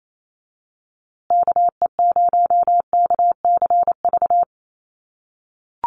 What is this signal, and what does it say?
Morse code, heard as a keyed tone, about 28 wpm, 700 Hz. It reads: KE0KCV
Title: USA Call Signs Repeated 3x 28wpm